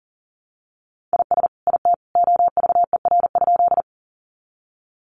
Morse code, heard as a keyed tone, about 40 wpm, 700 Hz. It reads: IS IT OVER?
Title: Sentences from Top 100 Words 40wpm